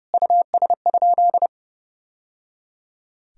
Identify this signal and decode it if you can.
Morse code, heard as a keyed tone, about 30 wpm, 700 Hz. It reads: US?
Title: Sentences from Top 1000 Words - Review Part 2 - 30wpm